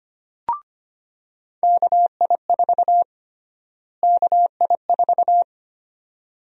Morse code, heard as a keyed tone, about 25 wpm, 700 Hz. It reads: KI4 KI4
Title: USA Call Sign Prefixes - Repeated 2x - 25wpm